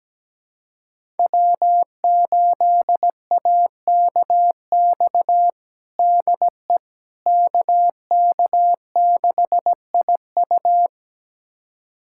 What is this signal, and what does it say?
Morse code, heard as a keyed tone, about 17 wpm, 700 Hz. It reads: W8AKX DE KK6IU
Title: USA Callsign DE Callsign - 5 Characters - 17wpm